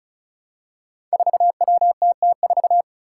Morse code, heard as a keyed tone, about 36 wpm, 700 Hz. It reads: 4WTT4
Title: Five Challenging Letter-Number Combinations 35wpm